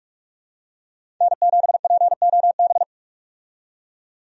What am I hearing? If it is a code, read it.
Morse code, heard as a keyed tone, about 46 wpm, 700 Hz. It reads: N7POB
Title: USA Call Signs - 5 Characters - 45wpm